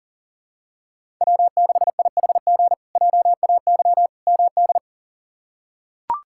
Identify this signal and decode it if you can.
Morse code, heard as a keyed tone, about 40 wpm, 700 Hz. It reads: W6IHG JAY MD
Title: K1USN SST QSO Practice - 40wpm